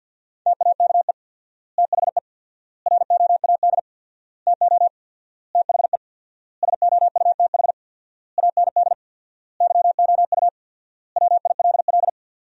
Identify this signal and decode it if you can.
Morse code, heard as a keyed tone, about 50 wpm, 700 Hz. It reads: TAKE THE ROAD TO THE SOUTH AND YOU WILL
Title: 4x Word-Spacing - Sentences from Top 400 Words - 50wpm